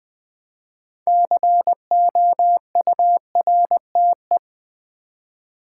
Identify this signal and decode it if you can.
Morse code, heard as a keyed tone, about 20 wpm, 700 Hz. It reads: COURTE
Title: Build A Word - 3 Letters At A Time - 8-Letter Words - 20wpm